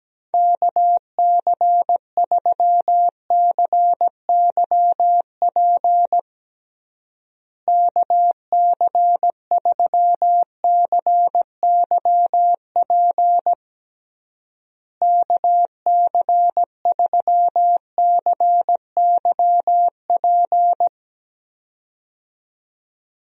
Morse code, heard as a keyed tone, about 17 wpm, 700 Hz. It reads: KC3CYP KC3CYP KC3CYP
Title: USA Call Signs - 6 Characters - Repeated 3x - 17wpm